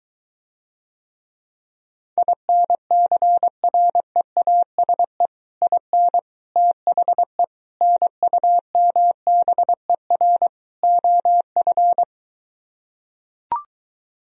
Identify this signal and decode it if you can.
Morse code, heard as a keyed tone, about 23 wpm, 700 Hz. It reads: INCREASE IN THE NUMBER OF